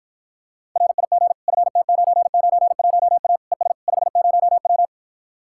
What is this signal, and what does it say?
Morse code, heard as a keyed tone, about 53 wpm, 700 Hz. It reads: RIG FT991A ES 50W